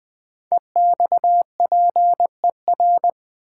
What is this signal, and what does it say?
Morse code, heard as a keyed tone, about 20 wpm, 700 Hz. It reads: EXPER
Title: Build A Word - 5 Letters At A Time - 12-Letter Words - 20wpm